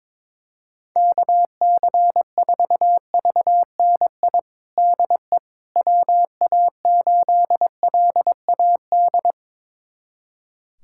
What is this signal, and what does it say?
Morse code, heard as a keyed tone, about 22 wpm, 700 Hz. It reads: KC4VNI DE WA8LAD
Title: USA Callsign DE Callsign 22wpm